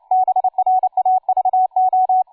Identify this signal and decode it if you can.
Morse code, heard as a keyed tone, about 29 wpm, 750 Hz. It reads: BRAVO
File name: cw forum / tu2 f5ldy 18mhz 16h46